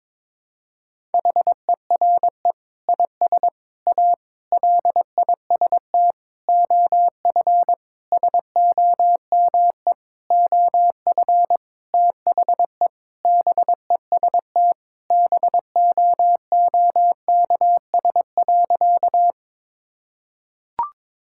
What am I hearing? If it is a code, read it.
Morse code, heard as a keyed tone, about 21 wpm, 700 Hz. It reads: HERE IS A LIST OF SOME OF THE BEST BOOKS.
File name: practice / 22wpm / Sentences with 8-Word Phrases - 22wpm